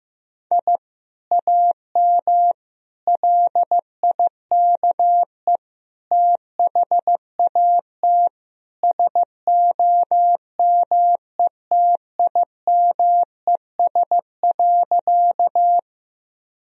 Morse code, heard as a keyed tone, about 16 wpm, 700 Hz. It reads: I AM LIKE THAT SOMETIMES.
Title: Sentences from Top 1000 Words - Review Part 3 - 15wpm